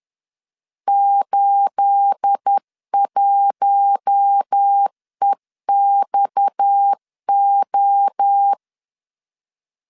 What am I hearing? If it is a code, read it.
Morse code, heard as a keyed tone, about 10 wpm, 800 Hz. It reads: 81EXO
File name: CW / DET / 1 9 1500C S B 10wpm 4439